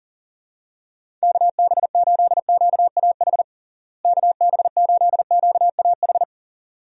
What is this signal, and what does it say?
Morse code, heard as a keyed tone, about 40 wpm, 700 Hz. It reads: KB8QAH KB8QAH
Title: USA Call Signs - 6 Characters - Repeated 2x - 40wpm